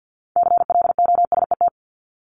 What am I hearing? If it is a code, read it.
Morse code, heard as a keyed tone, about 50 wpm, 700 Hz. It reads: CLOSET
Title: Top 1751-2000 Words 50wpm